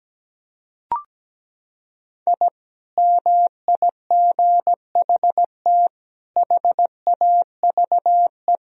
Morse code, heard as a keyed tone, about 17 wpm, 700 Hz. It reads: I MIGHT HAVE
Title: Sentences from Top 1000 Words - Review Part 3 - 17wpm